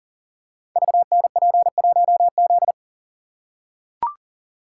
Morse code, heard as a keyed tone, about 39 wpm, 700 Hz. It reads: VNP1Z
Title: Code Groups - 5 Letter:Numbers - 40wpm